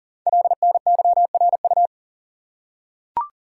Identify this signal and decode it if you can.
Morse code, heard as a keyed tone, about 39 wpm, 690 Hz. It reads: LNYRU
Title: Code Groups - 5 Letter:Numbers - 40wpm